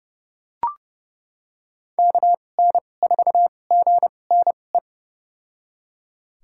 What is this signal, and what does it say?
Morse code, heard as a keyed tone, about 30 wpm, spaced as for 22 wpm, 700 Hz. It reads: KN4GNE
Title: USA Call Signs - 6 Characters - 2x Character Spacing - 30wpm